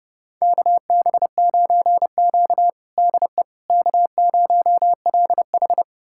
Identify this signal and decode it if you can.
Morse code, heard as a keyed tone, about 30 wpm, 700 Hz. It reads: KB9Q DE K0LH
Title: USA Callsign DE Callsign - 4 Characters - 30wpm